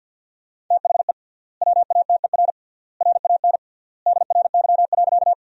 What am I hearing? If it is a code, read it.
Morse code, heard as a keyed tone, about 50 wpm, 700 Hz. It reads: THE WATER RAN DRY.
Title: 3x Word-Spacing - Sentences from Top 500 Words - 50wpm